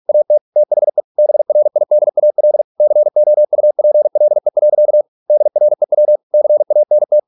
Morse code, heard as a keyed tone, about 46 wpm, 600 Hz. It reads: AT THE BRIDAL COUPLE. DREW CANT